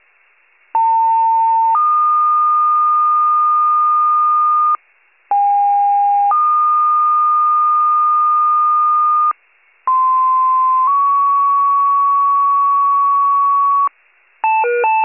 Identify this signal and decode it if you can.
Two-tone page, heard as a one-second tone, then a three-second tone, about 900 then 1220 Hz, about 810 then 1220 Hz, about 1040 then 1120 Hz. It is heard from 0.7 to 13.9 s.